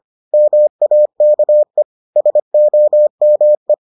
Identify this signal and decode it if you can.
Morse code, heard as a keyed tone, about 25 wpm, 600 Hz. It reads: MAKE SOME